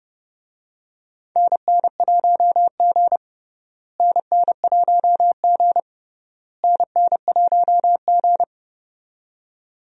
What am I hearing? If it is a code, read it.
Morse code, heard as a keyed tone, about 30 wpm, 700 Hz. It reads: NN1G NN1G NN1G
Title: USA Call Signs - 4 Characters - Repeated 3x - 30wpm